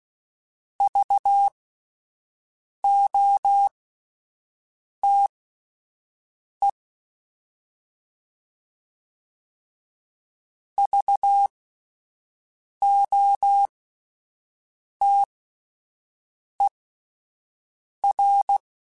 Morse code, heard as a keyed tone, about 16 wpm, spaced as for 5 wpm, 800 Hz. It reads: VOTE VOTER